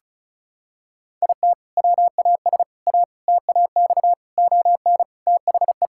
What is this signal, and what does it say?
Morse code, heard as a keyed tone, about 36 wpm, 700 Hz. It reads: IT WAS A TAX ON THE
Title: Sentences from Top 1000 Words - Review Part 5 - 35wpm